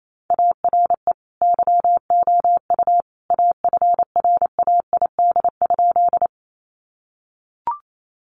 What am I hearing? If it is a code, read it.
Morse code, heard as a keyed tone, about 28 wpm, 700 Hz. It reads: ARE YOU AFRAID?
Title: Sentences from Top 1500 Words 28wpm